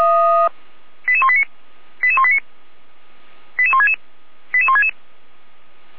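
Five-tone selective call (ZVEI), heard as quick runs of five tones, tones of about 70 ms. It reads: 88189 88189 88166 88166
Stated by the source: Five-tone ZVEI selective call 88169 received by SDR, archived by Feuerwehr Orlamünde (fire brigade).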